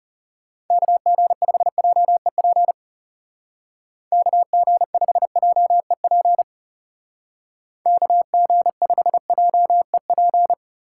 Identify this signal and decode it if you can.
Morse code, heard as a keyed tone, about 35 wpm, 690 Hz. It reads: KG5JEP KG5JEP KG5JEP
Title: Speed Racing - USA Call Signs 30wpm